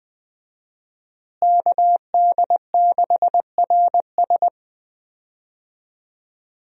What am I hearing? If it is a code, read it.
Morse code, heard as a keyed tone, about 20 wpm, 700 Hz. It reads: KD6RS
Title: USA Call Signs - 5 Characters - Repeated 3x - 20wpm